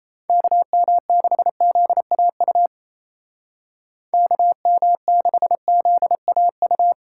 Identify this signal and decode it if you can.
Morse code, heard as a keyed tone, about 37 wpm, 700 Hz. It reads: KM6ZAU KM6ZAU